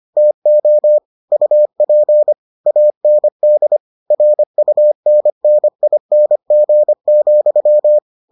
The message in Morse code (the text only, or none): TO UP AND RUNNING,